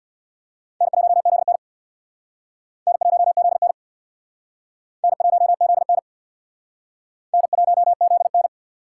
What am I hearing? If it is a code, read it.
Morse code, heard as a keyed tone, about 61 wpm, 700 Hz. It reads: N1ZN N1ZN N1ZN N1ZN
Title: Speed Racing - USA 1x2 Call Signs 50wpm